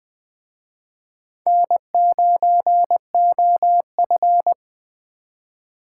Morse code, heard as a keyed tone, about 20 wpm, 700 Hz. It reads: N9OF
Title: Speed Racing - USA Call Signs 17wpm